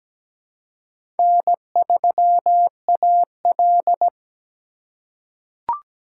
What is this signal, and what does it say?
Morse code, heard as a keyed tone, about 17 wpm, 700 Hz. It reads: N3AL
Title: USA Call Signs - 4 Characters - 3x Character Spacing - 17wpm